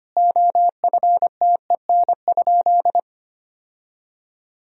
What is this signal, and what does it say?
Morse code, heard as a keyed tone, about 25 wpm, 700 Hz. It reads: OFTEN?
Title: Sentences from Top 1000 Words - Review Part 4 - 25wpm